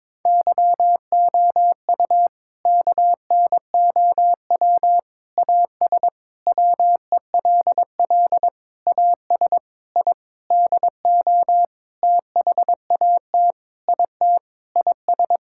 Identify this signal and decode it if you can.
Morse code, heard as a keyed tone, about 22 wpm, 700 Hz. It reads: YOU KNOW AS WELL AS I DO THAT IT IS
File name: practice / 22wpm / Sentences with 8-Word Phrases - 22wpm